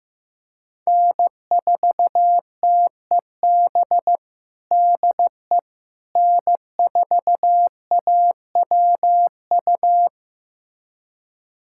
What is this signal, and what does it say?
Morse code, heard as a keyed tone, about 15 wpm, 700 Hz. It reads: N4TEB DE N4AWU
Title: USA Callsign DE Callsign - 5 Characters - 15wpm